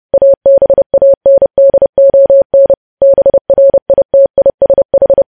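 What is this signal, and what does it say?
Morse code, heard as a keyed tone, about 30 wpm, 550 Hz. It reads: ABANDON BRITISH
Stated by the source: News headlines sent as Morse practice at 30.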